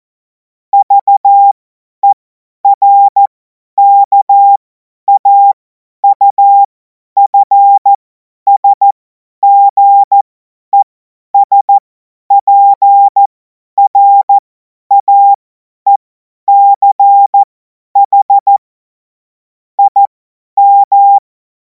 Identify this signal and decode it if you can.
Morse code, heard as a keyed tone, about 14 wpm, spaced as for 10 wpm, 800 Hz. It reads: VERKAUFSGESPRAECH IM
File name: Morsekurs / Witze / Witz 17. Februar 2026 50 F70